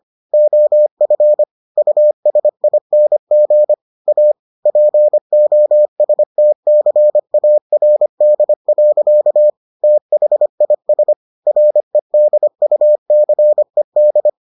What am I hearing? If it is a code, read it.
Morse code, heard as a keyed tone, about 25 wpm, 600 Hz. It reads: OF USING A POSTCARD. THIS REDUCED